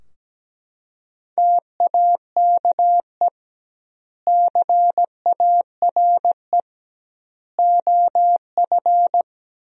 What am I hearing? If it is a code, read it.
Morse code, heard as a keyed tone, about 17 wpm, 700 Hz. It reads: TAKE CARE OF
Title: 2x Word-Spacing - Sentences from Top 300 Words - 17wpm